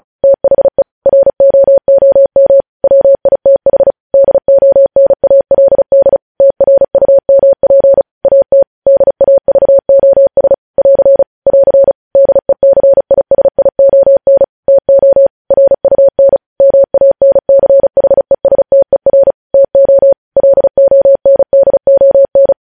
Morse code, heard as a keyed tone, about 36 wpm, 550 Hz. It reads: THE ROOM WITH DONALD TRUMP AT DAVOS <AR> <AR> DECISION TO RUN MANCHESTER TO LONDON
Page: News Headlines in Morse Code at 35 WPM